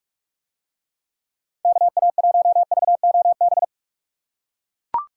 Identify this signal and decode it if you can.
Morse code, heard as a keyed tone, about 44 wpm, 700 Hz. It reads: KA1VOB